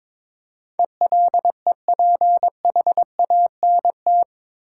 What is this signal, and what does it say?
Morse code, heard as a keyed tone, about 22 wpm, 700 Hz. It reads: ELEPHANT
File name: practice / 22wpm / Eight Letter Words - 22wpm